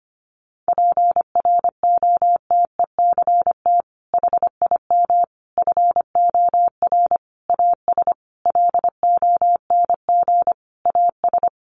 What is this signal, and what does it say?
Morse code, heard as a keyed tone, about 25 wpm, 700 Hz. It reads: PROTECT HIM FOR AS LONG AS
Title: Sentences from Top 900 Words 25wpm